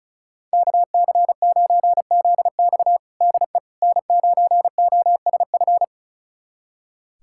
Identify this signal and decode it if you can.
Morse code, heard as a keyed tone, about 34 wpm, 690 Hz. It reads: KC9ZX DE N9OSF
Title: USA Callsign DE Callsign - 5 Characters - 35wpm